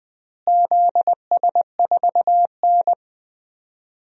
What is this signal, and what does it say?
Morse code, heard as a keyed tone, about 21 wpm, 700 Hz. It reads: ZS4N
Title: DXCC-100 Call Signs - 4 Characters - Repeated 3x - 20wpm